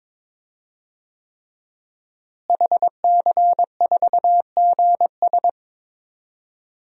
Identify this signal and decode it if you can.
Morse code, heard as a keyed tone, about 22 wpm, 700 Hz. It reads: HC4GS
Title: DXCC-100 Call Signs - 5 Characters - Repeated 3x - 22wpm